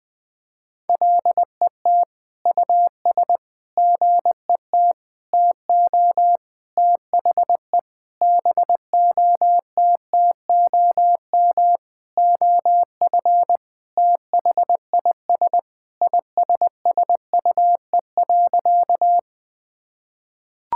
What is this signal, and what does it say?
Morse code, heard as a keyed tone, about 21 wpm, 700 Hz. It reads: LET US GET TO THE BOTTOM OF THIS ISSUE.
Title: Sentences with 7-Word Phrases - 20wpm